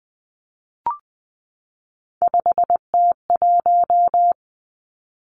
Morse code, heard as a keyed tone, about 20 wpm, 700 Hz. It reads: 5T1
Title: Three Challenging Letter-Number Combinations 20wpm